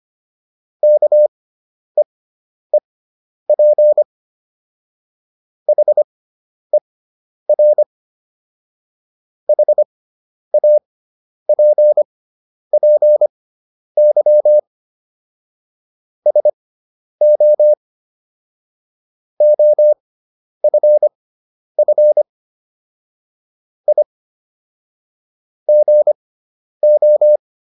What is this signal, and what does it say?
Morse code, heard as a keyed tone, about 25 wpm, spaced as for 10 wpm, 600 Hz. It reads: KEEP HER HAPPY SO OFF I GO